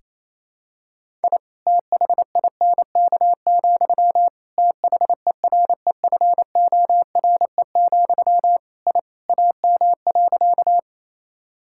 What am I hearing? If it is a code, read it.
Morse code, heard as a keyed tone, about 28 wpm, 700 Hz. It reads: I THINK, THEREFORE, I AM.
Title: Sentences from Top 1000 Words - Review Part 5 - 28wpm